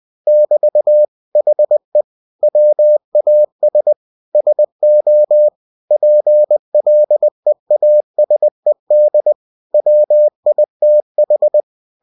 Morse code, heard as a keyed tone, about 20 wpm, 600 Hz. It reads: <BT> HE WAS SO PLEASED WITH
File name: cw-books / winnie-the-pooh / winnie-20wpm-page-0076